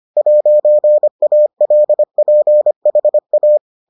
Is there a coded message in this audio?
Yes: Morse code from 0.2 to 3.6 s, heard as a keyed tone, about 25 wpm, 600 Hz. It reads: 'ALPHA